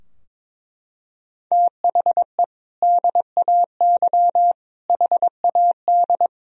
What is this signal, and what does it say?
Morse code, heard as a keyed tone, about 22 wpm, 700 Hz. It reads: THE DAY HAD
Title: Sentences from Top 1000 Words - Review Part 2 - 22wpm